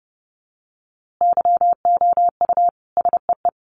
Morse code, heard as a keyed tone, about 30 wpm, 700 Hz. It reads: YOU SEE
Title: A Sound Of Thunder 30wpm